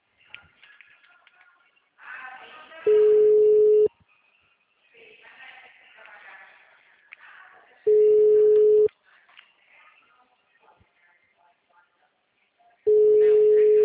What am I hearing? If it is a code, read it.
Ringback tone (Europe, 425 Hz), with speech.